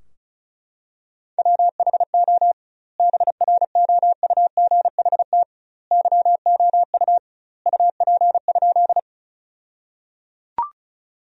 Morse code, heard as a keyed tone, about 36 wpm, 700 Hz. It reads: WHO BROUGHT YOU UP?
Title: 2x Word-Spacing - Sentences from Top 500 Words - 35wpm